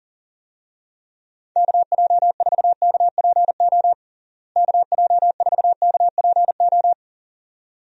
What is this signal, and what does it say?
Morse code, heard as a keyed tone, about 40 wpm, 700 Hz. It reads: KJ4KPO KJ4KPO
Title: USA Call Signs - 6 Characters - Repeated 2x - 40wpm